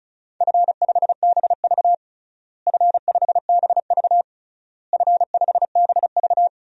Morse code, heard as a keyed tone, about 34 wpm, 700 Hz. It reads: F5BV F5BV F5BV
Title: DXCC-100 Call Signs - 4 Characters - Repeated 3x - 35wpm